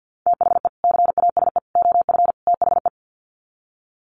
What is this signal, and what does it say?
Morse code, heard as a keyed tone, about 50 wpm, 700 Hz. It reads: THE CASE OF THE
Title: Sets of 4 Words 50wpm